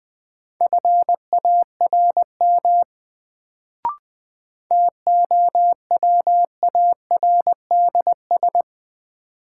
Morse code, heard as a keyed tone, about 21 wpm, 700 Hz. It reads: FARM TOWARDS
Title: Sentences from Top 1000 Words - Review Part 2 - New Word Drill - 20wpm